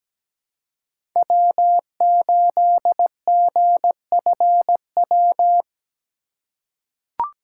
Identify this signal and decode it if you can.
Morse code, heard as a keyed tone, about 17 wpm, 700 Hz. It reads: W8GFW
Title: USA Call Signs - 5 Characters - 6x Character Spacing - 17wpm